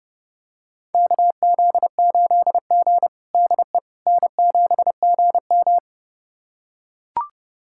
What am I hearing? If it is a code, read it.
Morse code, heard as a keyed tone, about 30 wpm, 700 Hz. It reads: KZ8G DE N7GM
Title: USA Callsign DE Callsign - 4 Characters - 30wpm